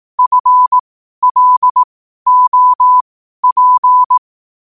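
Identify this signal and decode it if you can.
Morse code, heard as a keyed tone, about 18 wpm, spaced as for 12 wpm, 1000 Hz. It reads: FLOP
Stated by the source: Morse code quotes at 12 words per minute, by ai4qr.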